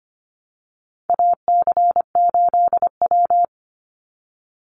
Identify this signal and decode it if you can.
Morse code, heard as a keyed tone, about 25 wpm, 700 Hz. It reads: AC8W
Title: Call Signs Repeated 3x - 25wpm